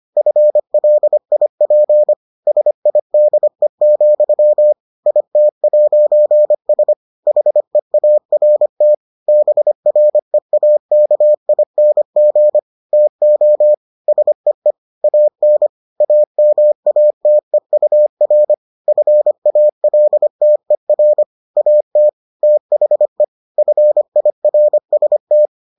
Morse code, heard as a keyed tone, about 25 wpm, 600 Hz. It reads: FLIP SIDE, IT'S HEART BREAKING TO SEE AN AMATEUR FALTER AT THE FIRST